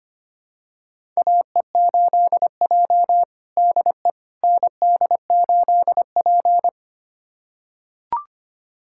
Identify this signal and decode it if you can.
Morse code, heard as a keyed tone, about 25 wpm, 700 Hz. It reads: AE8J DE ND8P
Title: USA Callsign DE Callsign - 4 Characters - 25wpm